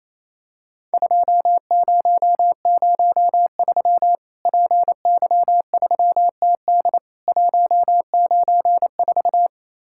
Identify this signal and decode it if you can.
Morse code, heard as a keyed tone, about 28 wpm, 700 Hz. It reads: 2003 PY3TD 194
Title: Worked All Europe DX Contest - QTC Exchange - 28wpm